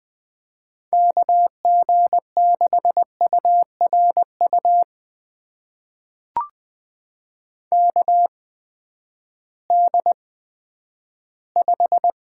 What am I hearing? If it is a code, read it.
Morse code, heard as a keyed tone, about 20 wpm, spaced as for 11 wpm, 700 Hz. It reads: KG6URU KD5
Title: USA Call Signs - 6 Characters - 8x Character Spacing - 20wpm